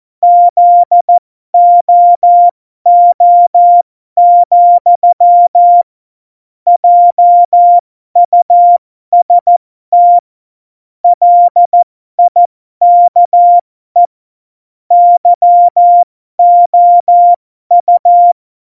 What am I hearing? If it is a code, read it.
Morse code, heard as a keyed tone, about 14 wpm, spaced as for 12 wpm, 700 Hz. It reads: ZOO, JUST LIKE YOU